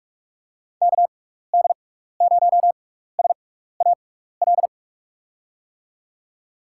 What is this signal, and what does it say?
Morse code, heard as a keyed tone, about 44 wpm, spaced as for 15 wpm, 700 Hz. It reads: KD0SAL
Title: USA Call Signs - 6 Characters - 6x Character Spacing - 45wpm